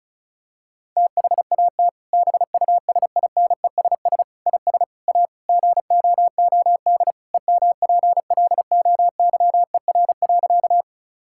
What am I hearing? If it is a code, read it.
Morse code, heard as a keyed tone, about 36 wpm, 700 Hz. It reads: THAT BUSINESS IS A GOOD EMPLOYER.
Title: Sentences from Top 1600 Words 35wpm